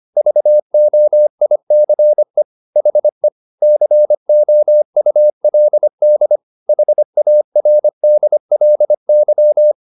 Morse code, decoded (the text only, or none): VOICE HE COULD HARDLY